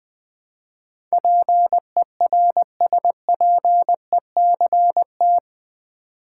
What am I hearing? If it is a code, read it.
Morse code, heard as a keyed tone, about 20 wpm, 700 Hz. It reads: PERSPECT
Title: Build A Word - 4 Letters At A Time - 11-Letter Words - 20wpm